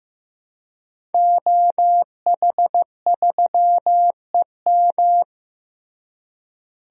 Morse code, heard as a keyed tone, about 15 wpm, 700 Hz. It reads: OH3EM
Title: DXCC-100 Call Signs - 5 Characters - Repeated 3x - 15wpm